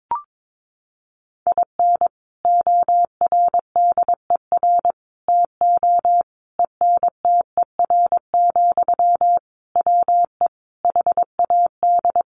Morse code, heard as a keyed tone, about 22 wpm, 700 Hz. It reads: IN ORDER TO ENTER, WE HAD
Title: Sentences from Top 1300 Words 22wpm